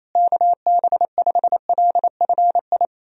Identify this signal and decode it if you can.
Morse code, heard as a keyed tone, about 28 wpm, 700 Hz. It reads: KB5LFI
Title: USA Call Signs - 6 Characters - Repeated 3x - 28wpm